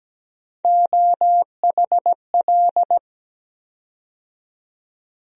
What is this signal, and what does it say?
Morse code, heard as a keyed tone, about 17 wpm, 700 Hz. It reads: OHL
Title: USA Call Sign Postfixes - 3 Letters 17wpm